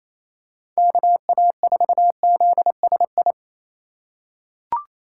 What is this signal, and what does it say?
Morse code, heard as a keyed tone, about 28 wpm, 700 Hz. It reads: KA4ZSI
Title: USA Call Signs - 6 Characters - 2x Character Spacing - 28wpm